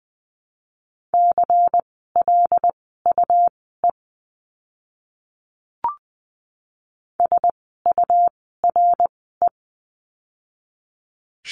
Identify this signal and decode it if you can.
Morse code, heard as a keyed tone, about 20 wpm, spaced as for 15 wpm, 700 Hz. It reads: CLUE SURE
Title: 101 Lesson 17f - Letter u - Words 20wpm